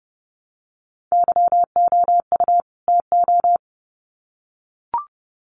Morse code, heard as a keyed tone, about 29 wpm, 700 Hz. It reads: YOU TO
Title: Sets of 2 Words 30wpm